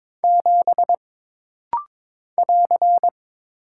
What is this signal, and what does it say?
Morse code, heard as a keyed tone, about 22 wpm, 700 Hz. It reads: 7 <AR>
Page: This rapid-fire timing is geared to push your ICR (Instant Character Recognition) skills.